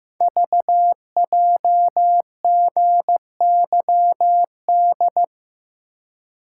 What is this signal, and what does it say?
Morse code, heard as a keyed tone, about 15 wpm, 700 Hz. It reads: VJGYD
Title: Code Groups - 5 Letters - 15wpm